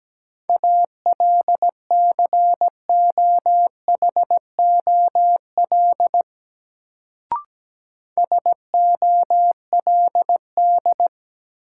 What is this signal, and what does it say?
Morse code, heard as a keyed tone, about 17 wpm, 700 Hz. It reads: ALCOHOL SOLD
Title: Sentences from Top 1400 Words - New Word Drill - 17wpm